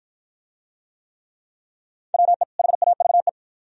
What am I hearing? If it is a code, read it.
Morse code, heard as a keyed tone, about 53 wpm, 700 Hz. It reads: WE HAVE